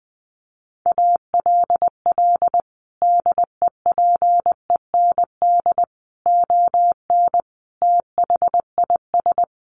Morse code, heard as a keyed tone, about 19 wpm, 700 Hz. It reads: ALL DEPEND ON THIS
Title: Sentences from Top 1300 Words 20wpm